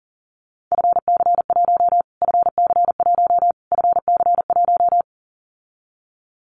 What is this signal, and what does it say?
Morse code, heard as a keyed tone, about 40 wpm, 700 Hz. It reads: FC1 FC1 FC1
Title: Callsign Prefixes for Top 25 DXCC - Repeated 3x - 40wpm